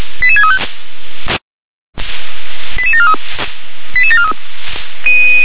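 Five-tone selective call (ZVEI), heard as quick runs of five tones, tones of about 70 ms. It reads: 88635 88743 88743
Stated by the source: ZVEI five-tone selective call received on an SDR.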